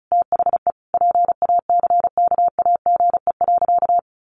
Morse code, heard as a keyed tone, about 34 wpm, 700 Hz. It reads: THE PACKAGE.
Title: Additional Easy Phrases 35wpm